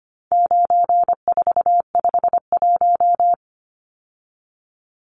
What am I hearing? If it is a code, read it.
Morse code, heard as a keyed tone, about 25 wpm, 700 Hz. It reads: 9451
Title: Numbers - 4-Digit 25wpm